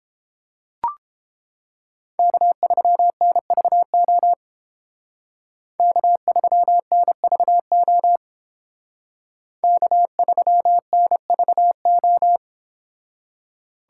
Morse code, heard as a keyed tone, about 32 wpm, 700 Hz. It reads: K3NVO K3NVO K3NVO